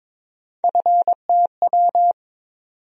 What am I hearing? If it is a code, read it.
Morse code, heard as a keyed tone, about 22 wpm, 700 Hz. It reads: FTW
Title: USA Call Sign Postfixes - 3 Letters 22wpm